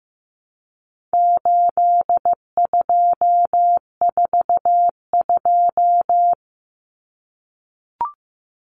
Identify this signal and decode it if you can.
Morse code, heard as a keyed tone, about 15 wpm, 700 Hz. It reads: 8242
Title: Common Cut Numbers - 4-Digit 15wpm